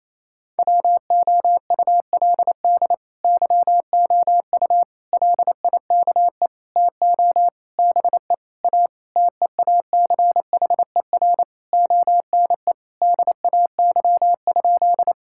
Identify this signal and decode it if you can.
Morse code, heard as a keyed tone, about 28 wpm, 700 Hz. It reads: WOULD YOU LIKE TO BE A TEACHER ONE DAY?
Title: Sentences from Top 1000 Words - Review Part 6 - 28wpm